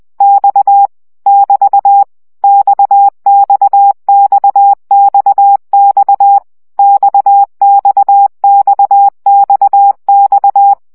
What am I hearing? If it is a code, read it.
Morse code, heard as a keyed tone, about 20 wpm, 800 Hz. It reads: X <BT> XXXXX XXXXX
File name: CW gyakorlas x 17 baud